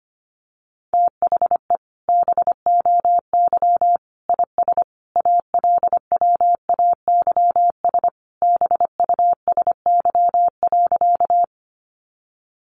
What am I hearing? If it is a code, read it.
Morse code, heard as a keyed tone, about 25 wpm, 700 Hz. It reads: THE BOY IS ALWAYS BUSY.
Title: Additional Easy Phrases 25wpm